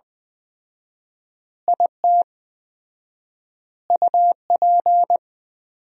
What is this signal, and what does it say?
Morse code, heard as a keyed tone, about 20 wpm, 700 Hz. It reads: IT UP
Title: 4x Word-Spacing - Sentences from Top 400 Words - 20wpm